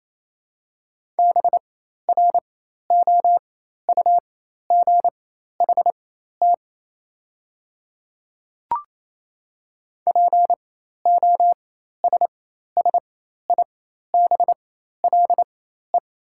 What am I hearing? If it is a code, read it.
Morse code, heard as a keyed tone, about 28 wpm, spaced as for 13 wpm, 700 Hz. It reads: BROUGHT POSSIBLE
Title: 4x Character Spacing - Top 500 Words - 28wpm